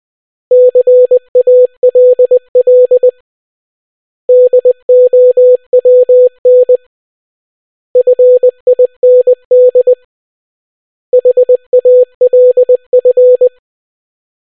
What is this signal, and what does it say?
Morse code, heard as a keyed tone, about 20 wpm, 500 Hz. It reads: CALL DOWN FIND HALF